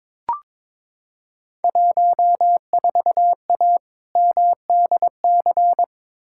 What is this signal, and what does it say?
Morse code, heard as a keyed tone, about 22 wpm, 700 Hz. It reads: 14A MDC
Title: Field Day Exchanges Encore - 22wpm